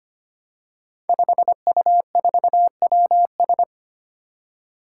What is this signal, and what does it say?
Morse code, heard as a keyed tone, about 25 wpm, 700 Hz. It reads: 5U4WS
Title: Five Challenging Letter-Number Combinations 25wpm